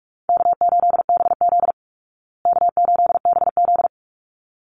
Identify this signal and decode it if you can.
Morse code, heard as a keyed tone, about 44 wpm, 700 Hz. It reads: K8BZ K8BZ
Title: Easy Call Signs Repeated 2x - 45wpm